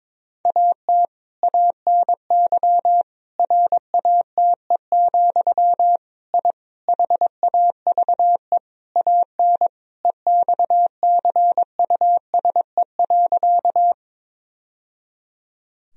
Morse code, heard as a keyed tone, about 22 wpm, 700 Hz. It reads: AT ANY RATE, I HAVE AN EXCUSE.
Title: Sentences from Top 1000 Words - Review Part 4 - 22wpm